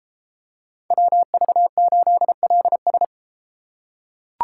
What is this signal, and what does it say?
Morse code, heard as a keyed tone, about 33 wpm, 700 Hz. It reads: WV8LS